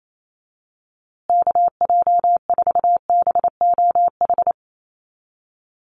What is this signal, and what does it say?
Morse code, heard as a keyed tone, about 28 wpm, 700 Hz. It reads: KJ4BOH
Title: USA Call Signs Repeated 2x 28wpm